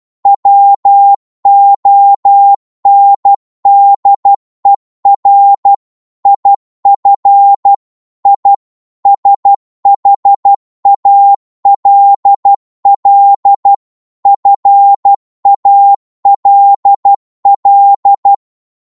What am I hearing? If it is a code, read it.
Morse code, heard as a keyed tone, about 12 wpm, 800 Hz. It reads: WONDER IF I SHALL FALL
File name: Morse / psg-12wpm